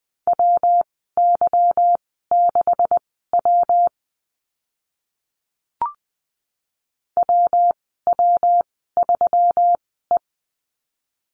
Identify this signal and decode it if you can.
Morse code, heard as a keyed tone, about 20 wpm, spaced as for 15 wpm, 700 Hz. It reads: WY6W WW3E
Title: 147 Lesson 24g - Letter y - Call Signs 20wpm